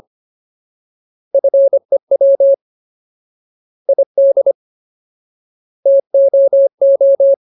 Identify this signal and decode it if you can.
Morse code, heard as a keyed tone, about 25 wpm, 550 Hz. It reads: FEW ID TOO
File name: Ultimate Word List / 25WPM Ultimate